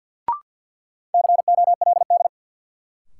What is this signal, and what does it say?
Morse code, heard as a keyed tone, about 53 wpm, 700 Hz. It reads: COLD
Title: Sets of 5 Words - New Word Drill - 50wpm